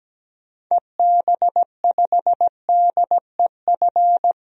Morse code, heard as a keyed tone, about 17 wpm, 700 Hz. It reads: EB5DEF
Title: DXCC-100 Call Signs - 6 Characters - Repeated 3x - 17wpm